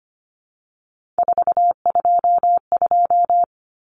Morse code, heard as a keyed tone, about 25 wpm, 700 Hz. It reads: 422
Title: Common Cut Numbers - 3-Digit 25wpm